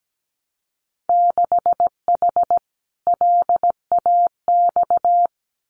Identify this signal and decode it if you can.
Morse code, heard as a keyed tone, about 17 wpm, 700 Hz. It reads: 6H LAX
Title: Winter Field Day Exchanges Encore 17wpm